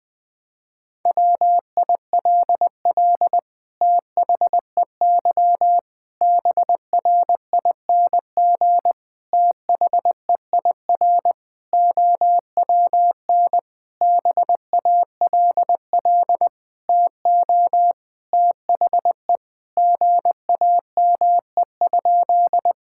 Morse code, heard as a keyed tone, about 20 wpm, 700 Hz. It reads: WILL THEY BRING THEIR OWN BALL TO THE GAME?
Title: Sentences from Top 1000 Words - Review Part 6 - 20wpm